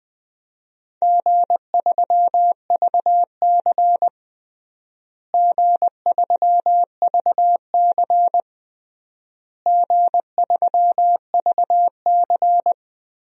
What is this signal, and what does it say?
Morse code, heard as a keyed tone, about 20 wpm, 700 Hz. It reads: G3VC G3VC G3VC
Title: DXCC-100 Call Signs - 4 Characters - Repeated 3x - 20wpm